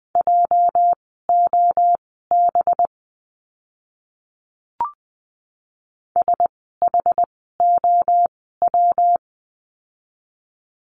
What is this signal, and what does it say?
Morse code, heard as a keyed tone, about 20 wpm, spaced as for 15 wpm, 700 Hz. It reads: JOB SHOW
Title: 206 Lesson 33f - Letter j - Words 20wpm